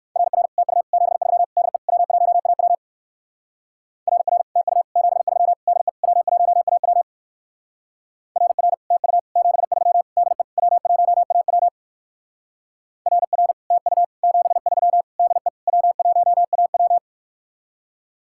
Morse code, heard as a keyed tone, about 59 wpm, 700 Hz. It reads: RR TU 73 DE W1AW RR TU 73 DE W1AW RR TU 73 DE W1AW RR TU 73 DE W1AW